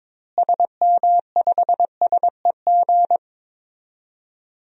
Morse code, heard as a keyed tone, about 21 wpm, 700 Hz. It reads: SM5SEG
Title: DXCC-100 Call Signs - 6 Characters - Repeated 3x - 22wpm